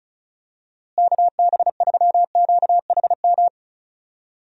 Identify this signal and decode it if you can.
Morse code, heard as a keyed tone, about 34 wpm, 700 Hz. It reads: KB3QHM
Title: USA Call Signs - 6 Characters - Repeated 3x - 35wpm